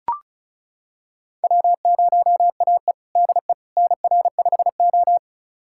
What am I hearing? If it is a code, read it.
Morse code, heard as a keyed tone, about 36 wpm, 700 Hz. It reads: W0AE DE NR5O
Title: USA Callsign DE Callsign - 4 Characters - 35wpm